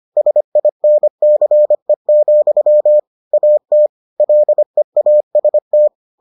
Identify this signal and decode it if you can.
Morse code, heard as a keyed tone, about 25 wpm, 600 Hz. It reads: SINCE, AT LEAST